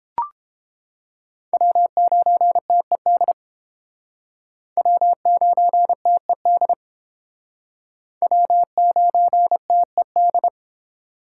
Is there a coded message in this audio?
Yes: Morse code from 1.5 to 10.5 s, heard as a keyed tone, about 31 wpm, 700 Hz. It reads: W9TED W9TED W9TED